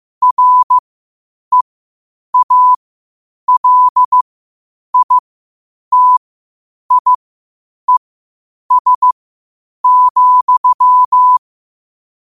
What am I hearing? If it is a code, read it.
Morse code, heard as a keyed tone, about 15 wpm, spaced as for 8 wpm, 1000 Hz. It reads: REALITIES,
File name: qotdfiles / 8wpm